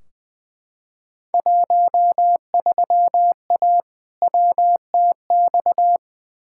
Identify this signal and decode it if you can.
Morse code, heard as a keyed tone, about 20 wpm, 700 Hz. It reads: 13A WTX
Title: Field Day Exchanges Encore - 20wpm